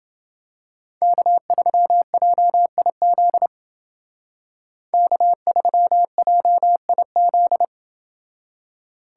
Morse code, heard as a keyed tone, about 29 wpm, 700 Hz. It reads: K3JIZ K3JIZ